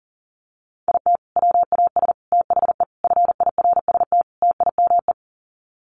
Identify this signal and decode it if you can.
Morse code, heard as a keyed tone, about 40 wpm, 700 Hz. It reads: IT WAS THE FIRST TIME
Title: Sets of 5 Words 40wpm